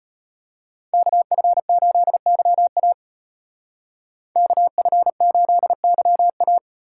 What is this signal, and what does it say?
Morse code, heard as a keyed tone, about 34 wpm, 700 Hz. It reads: KF8YA KF8YA